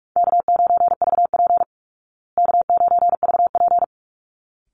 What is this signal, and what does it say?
Morse code, heard as a keyed tone, about 44 wpm, 700 Hz. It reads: K9VP K9VP
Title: Easy Call Signs Repeated 2x - 45wpm